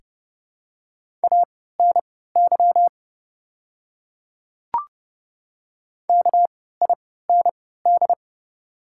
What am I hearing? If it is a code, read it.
Morse code, heard as a keyed tone, about 32 wpm, spaced as for 17 wpm, 700 Hz. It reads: ANY KIND
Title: 3x Character Spacing - Top 200 Words - 30wpm